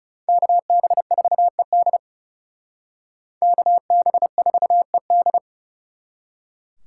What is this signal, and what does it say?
Morse code, heard as a keyed tone, about 38 wpm, 700 Hz. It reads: KB4ED KB4ED